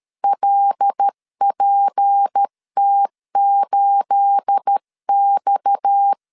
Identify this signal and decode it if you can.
Morse code, heard as a keyed tone, about 13 wpm, 800 Hz. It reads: LPT8X
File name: LEZIONI / 9 286 12wpm 2305